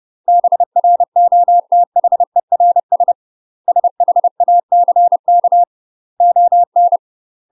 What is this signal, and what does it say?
Morse code, heard as a keyed tone, about 30 wpm, 700 Hz. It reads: BROTHERS SHACK ON
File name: UK MASTER QSOs TRAINING / 30WPM / UK MASTER QSOs 0036